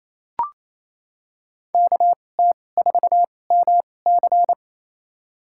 Morse code, heard as a keyed tone, about 28 wpm, spaced as for 20 wpm, 700 Hz. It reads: KT4MC
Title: USA Call Signs - 5 Characters - 2x Character Spacing - 28wpm